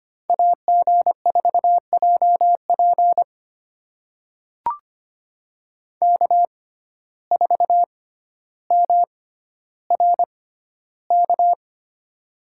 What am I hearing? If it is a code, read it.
Morse code, heard as a keyed tone, about 25 wpm, spaced as for 13 wpm, 700 Hz. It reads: AG4JP K4MRK
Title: USA Call Signs - 5 Characters - 6x Character Spacing - 25wpm